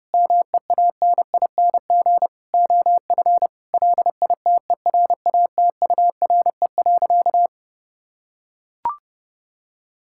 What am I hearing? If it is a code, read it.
Morse code, heard as a keyed tone, about 30 wpm, 700 Hz. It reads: MEANING OF LITERATURE.
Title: Sentences from Top 1000 Words - Review Part 6 - 30wpm